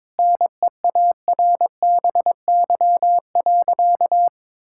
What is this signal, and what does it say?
Morse code, heard as a keyed tone, about 22 wpm, 700 Hz. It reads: NEARBY.
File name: practice / 22wpm / Sentences with 2-Word Phrases - 22wpm